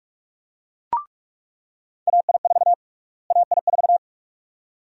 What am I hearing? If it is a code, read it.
Morse code, heard as a keyed tone, about 46 wpm, 700 Hz. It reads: AI4 AI4
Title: USA Call Sign Prefixes - Repeated 2x - 45wpm